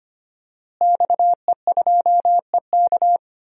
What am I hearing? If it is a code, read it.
Morse code, heard as a keyed tone, about 25 wpm, 700 Hz. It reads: XE2EK
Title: DXCC-100 Call Signs - 5 Characters - Repeated 3x - 25wpm